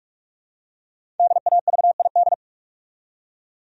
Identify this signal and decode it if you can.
Morse code, heard as a keyed tone, about 44 wpm, 700 Hz. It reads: DAVID
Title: USA Common Names - 45wpm